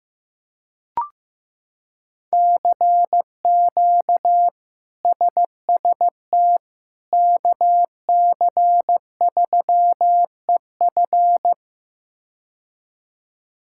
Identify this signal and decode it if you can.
Morse code, heard as a keyed tone, about 15 wpm, 700 Hz. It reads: CQ SST KC3EF
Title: K1USN SST QSO Practice - 15wpm